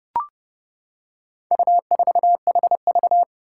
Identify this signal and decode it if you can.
Morse code, heard as a keyed tone, about 30 wpm, 700 Hz. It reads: U4HV
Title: Four Challenging Letter-Number Combinations 30wpm